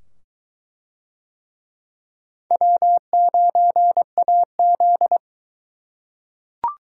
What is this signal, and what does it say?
Morse code, heard as a keyed tone, about 21 wpm, 700 Hz. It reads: W9AZ